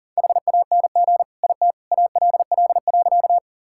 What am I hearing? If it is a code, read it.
Morse code, heard as a keyed tone, about 40 wpm, 700 Hz. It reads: HANG IT ALL.
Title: Sentences from Top 1600 Words 40wpm